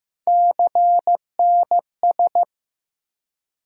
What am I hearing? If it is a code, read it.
Morse code, heard as a keyed tone, about 15 wpm, 700 Hz. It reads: CNS
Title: USA Call Sign Postfixes - 3 Letters 15wpm